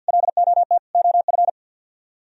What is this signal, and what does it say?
Morse code, heard as a keyed tone, about 64 wpm, 700 Hz. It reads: LOT OF